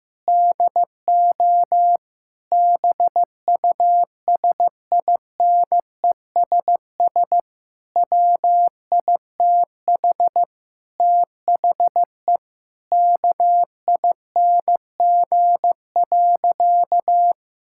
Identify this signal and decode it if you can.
Morse code, heard as a keyed tone, about 14 wpm, 700 Hz. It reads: DO BUSINESS WITH THE KING.
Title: Sentences from Top 1000 Words - Review Part 6 - 15wpm